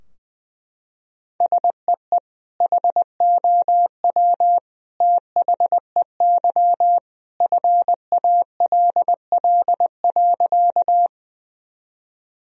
Morse code, heard as a keyed tone, about 20 wpm, 700 Hz. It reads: SEE HOW THEY FALL.
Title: Sentences from Top 1000 Words - Review Part 2 - 20wpm